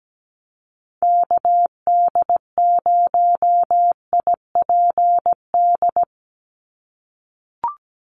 Morse code, heard as a keyed tone, about 17 wpm, 700 Hz. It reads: KD0IPD
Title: USA Call Signs 17wpm